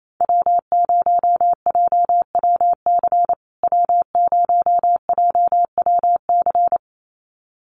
Morse code, heard as a keyed tone, about 28 wpm, 700 Hz. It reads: W0JWC W0JWC
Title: USA Call Signs Repeated 2x 28wpm